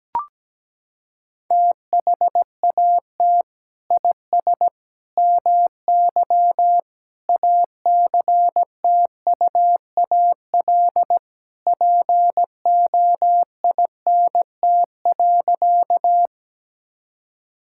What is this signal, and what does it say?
Morse code, heard as a keyed tone, about 17 wpm, 700 Hz. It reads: THAT IS MY ACTUAL POINT.
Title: Sentences from Top 1000 Words - Review Part 5 - 17wpm